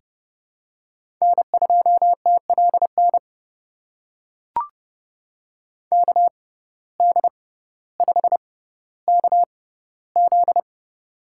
Morse code, heard as a keyed tone, about 30 wpm, spaced as for 15 wpm, 700 Hz. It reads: N2TLN KD5KZ
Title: USA Call Signs - 6x Character Spacing - 30wpm